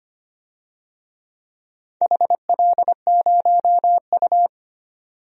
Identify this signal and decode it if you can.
Morse code, heard as a keyed tone, about 25 wpm, 700 Hz. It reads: HL0U
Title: DXCC-100 Call Signs - 4 Characters - 25wpm